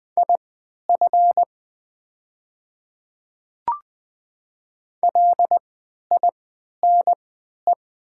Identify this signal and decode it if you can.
Morse code, heard as a keyed tone, about 20 wpm, spaced as for 11 wpm, 700 Hz. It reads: IF LINE
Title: 3x Character Spacing - Top 300 Words - 20wpm